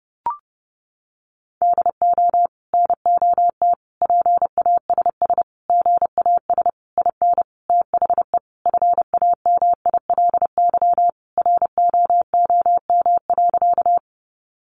Morse code, heard as a keyed tone, about 30 wpm, 700 Hz. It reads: DO NOT PASS GAS IN THE FAMILY ROOM.
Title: Sentences from Top 1100 Words 30wpm